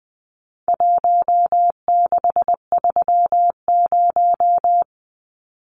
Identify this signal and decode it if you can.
Morse code, heard as a keyed tone, about 19 wpm, 700 Hz. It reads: 1630
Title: Numbers - Contest Style - 4-Digit 20wpm